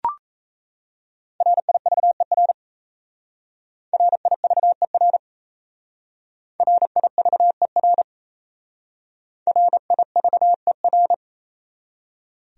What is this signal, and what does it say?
Morse code, heard as a keyed tone, about 34 wpm, 700 Hz. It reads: RIVER RIVER RIVER RIVER
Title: Speed-Racing Top 300 Words 28wpm